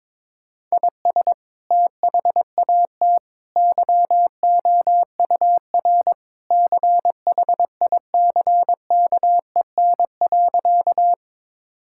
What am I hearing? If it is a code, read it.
Morse code, heard as a keyed tone, about 22 wpm, 700 Hz. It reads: IS THAT YOUR CHICKEN.
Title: Sentences from Top 1000 Words - Review Part 5 - 22wpm